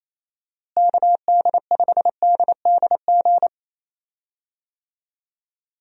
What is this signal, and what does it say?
Morse code, heard as a keyed tone, about 28 wpm, 700 Hz. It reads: KD5DDG
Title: USA Call Signs - 6 Characters - Repeated 2x - 28wpm